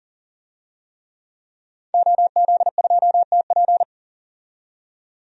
Morse code, heard as a keyed tone, about 39 wpm, 700 Hz. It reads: OZ2TP
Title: DXCC-100 Call Signs - 5 Characters - 40wpm